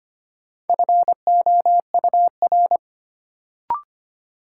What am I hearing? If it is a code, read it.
Morse code, heard as a keyed tone, about 25 wpm, 700 Hz. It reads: FOUR
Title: Sentences from Top 300 Words - New Word Drill - 25wpm